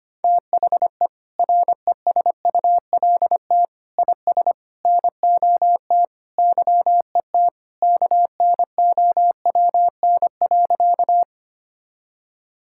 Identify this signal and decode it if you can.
Morse code, heard as a keyed tone, about 25 wpm, 700 Hz. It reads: THE RESULT IS NOT YET KNOWN.
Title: Sentences from Top 1000 Words - Review Part 5 - 25wpm